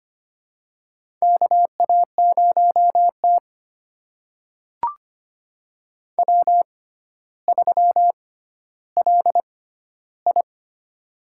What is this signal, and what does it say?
Morse code, heard as a keyed tone, about 25 wpm, spaced as for 13 wpm, 700 Hz. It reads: KA0T W3LI
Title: USA Call Signs - 4 Characters - 6x Character Spacing - 25wpm